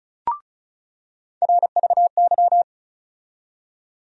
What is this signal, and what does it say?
Morse code, heard as a keyed tone, about 36 wpm, 700 Hz. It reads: RVY
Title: USA Call Sign Postfixes - 3 Letters 35wpm